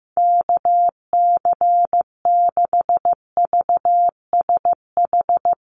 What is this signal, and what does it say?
Morse code, heard as a keyed tone, about 15 wpm, 700 Hz. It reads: KC6VSH
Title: USA Call Signs Repeated 2x 15wpm